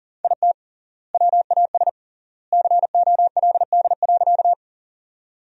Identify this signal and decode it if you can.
Morse code, heard as a keyed tone, about 40 wpm, 700 Hz. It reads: IT WAS COLD.
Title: 3x Word-Spacing - Sentences from Top 400 Words - 40wpm